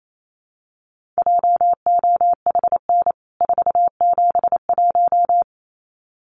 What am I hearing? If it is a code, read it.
Morse code, heard as a keyed tone, about 28 wpm, 700 Hz. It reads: JOHN 471
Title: CWT Exchange 28wpm